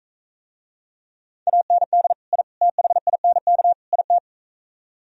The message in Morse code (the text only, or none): AND I THINK IT